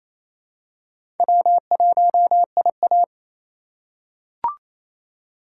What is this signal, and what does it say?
Morse code, heard as a keyed tone, about 28 wpm, 700 Hz. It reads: W1IA
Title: USA Call Signs - 4 Characters - 6x Character Spacing - 28wpm